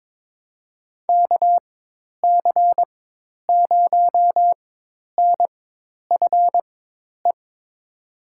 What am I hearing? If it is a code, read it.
Morse code, heard as a keyed tone, about 21 wpm, spaced as for 10 wpm, 700 Hz. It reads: KC0NFE
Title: USA Call Signs - 4x Character Spacing - 22wpm